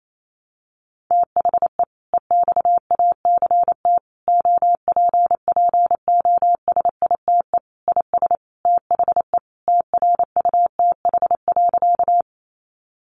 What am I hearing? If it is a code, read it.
Morse code, heard as a keyed tone, about 28 wpm, 700 Hz. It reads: THE EXACT OPPOSITE IS THE TRUTH.
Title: Sentences from Top 1300 Words 28wpm